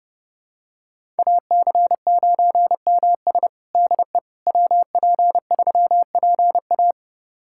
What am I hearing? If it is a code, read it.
Morse code, heard as a keyed tone, about 32 wpm, 700 Hz. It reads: AC9MS DE WP3PA
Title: USA Callsign DE Callsign - 5 Characters - 30wpm